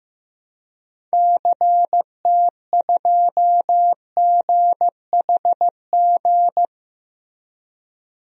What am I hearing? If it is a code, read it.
Morse code, heard as a keyed tone, about 16 wpm, 700 Hz. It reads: CT2GHG
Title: DXCC-100 Call Signs - 6 Characters - Repeated 3x - 15wpm